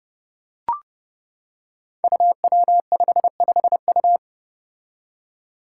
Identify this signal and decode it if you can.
Morse code, heard as a keyed tone, about 31 wpm, 700 Hz. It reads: UW55U
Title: Five Challenging Letter-Number Combinations 30wpm